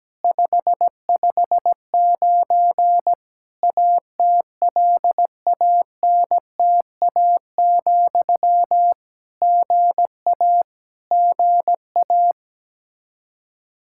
Morse code, heard as a keyed tone, about 17 wpm, 700 Hz. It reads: 559 ATLANTA, GA GA
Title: K1USN SST QSO Practice 17wpm